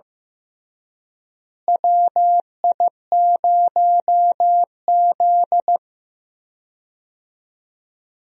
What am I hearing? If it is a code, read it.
Morse code, heard as a keyed tone, about 16 wpm, 700 Hz. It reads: WI0Z
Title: USA Call Signs - 4 Characters - Repeated 3x - 15wpm